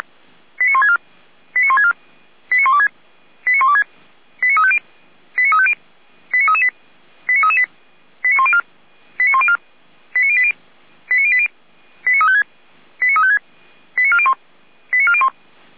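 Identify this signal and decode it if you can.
Five-tone selective call (ZVEI), heard as quick runs of five tones, tones of about 70 ms. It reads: 89164 89164 89126 89126 89350 89350 89338 89338 89104 89104 89080 89080 89356 89356 89401 89401
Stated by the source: ZVEI five-tone call 89701 received on an SDR, channel 455.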